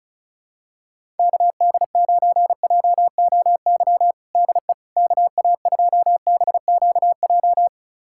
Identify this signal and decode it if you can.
Morse code, heard as a keyed tone, about 36 wpm, 700 Hz. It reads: KD9JOY DE KA2BQJ
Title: USA Callsign DE Callsign - 6 Characters - 35wpm